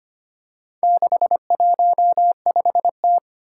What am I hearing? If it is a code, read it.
Morse code, heard as a keyed tone, about 25 wpm, 700 Hz. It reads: 615T
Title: Four Challenging Letter-Number Combinations 25wpm